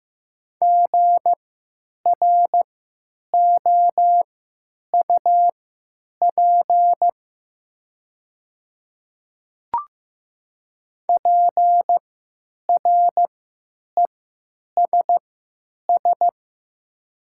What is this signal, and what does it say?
Morse code, heard as a keyed tone, about 15 wpm, spaced as for 9 wpm, 700 Hz. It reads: GROUP PRESS
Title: 3x Character Spacing - Top 300 Words - 15wpm